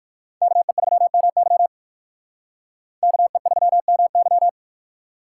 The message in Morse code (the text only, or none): KE3MY KE3MY